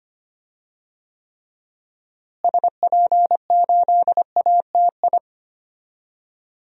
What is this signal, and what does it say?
Morse code, heard as a keyed tone, about 25 wpm, 700 Hz. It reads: SP8ATI
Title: DXCC-100 Call Signs - 6 Characters - Repeated 3x - 25wpm